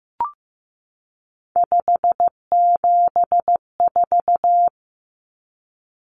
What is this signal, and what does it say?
Morse code, heard as a keyed tone, about 15 wpm, 700 Hz. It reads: 574
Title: Numbers - 3-Digit 15wpm